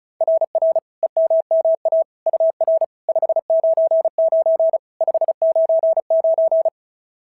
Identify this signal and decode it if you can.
Morse code, heard as a keyed tone, about 36 wpm, 690 Hz. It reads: RR EMMA UR 599 599
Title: SOTA QSO Practice 35wpm